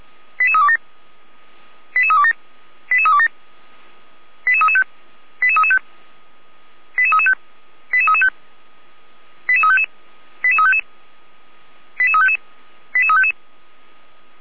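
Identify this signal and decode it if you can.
Five-tone selective call (ZVEI), heard as quick runs of five tones, tones of about 70 ms. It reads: 88327 88327 88327 88335 88335 88335 88335 88355 88355 88355 88355
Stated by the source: ZVEI five-tone selective call 88367 received on an SDR.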